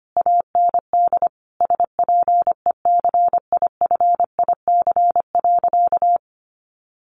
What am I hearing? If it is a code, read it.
Morse code, heard as a keyed tone, about 25 wpm, 700 Hz. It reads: AND SPECIFIC.
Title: Sentences from Top 1300 Words 25wpm